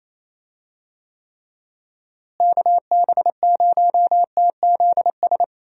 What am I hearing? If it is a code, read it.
Morse code, heard as a keyed tone, about 28 wpm, 700 Hz. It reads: KB0TZS
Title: USA Call Signs - 6 Characters - Repeated 3x - 28wpm